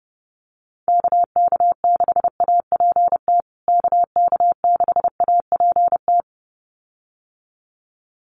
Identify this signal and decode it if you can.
Morse code, heard as a keyed tone, about 30 wpm, 700 Hz. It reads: KK6APT KK6APT
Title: USA Call Signs Repeated 2x 30wpm